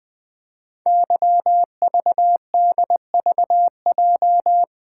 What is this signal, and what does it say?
Morse code, heard as a keyed tone, about 20 wpm, 700 Hz. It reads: YVDVJ
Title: Code Groups - 5 Letters - 20wpm